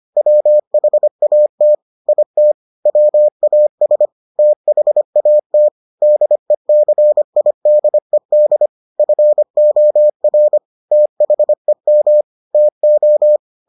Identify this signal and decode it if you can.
Morse code, heard as a keyed tone, about 25 wpm, 600 Hz. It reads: WHAT IT WAS THAT DECIDED FOR THEM TO